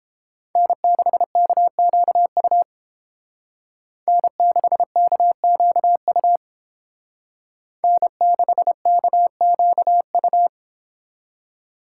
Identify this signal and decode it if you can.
Morse code, heard as a keyed tone, about 30 wpm, 700 Hz. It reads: N6KQU N6KQU N6KQU